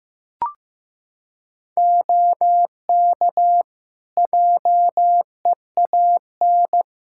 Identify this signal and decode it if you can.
Morse code, heard as a keyed tone, about 15 wpm, 700 Hz. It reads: OK JEAN
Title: Practice QSO - Exchange 4 - Ending - 15wpm